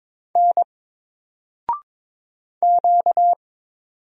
Morse code, heard as a keyed tone, about 22 wpm, 700 Hz. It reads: N Q
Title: Single Character (Letter:Number:Punctuation:Prosign) 22wpm